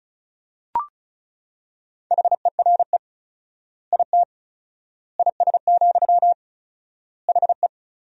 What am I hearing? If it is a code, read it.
Morse code, heard as a keyed tone, about 36 wpm, 700 Hz. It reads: HERE IT IS, HE
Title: 4x Word-Spacing - Sentences from Top 200 Words - 35wpm